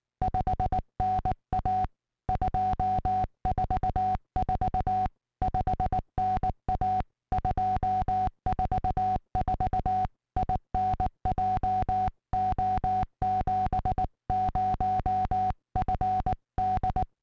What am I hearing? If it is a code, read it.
Morse code, heard as a keyed tone, about 19 wpm, 740 Hz, with noise. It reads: 5NA 244 5NA 244 IN JO70FD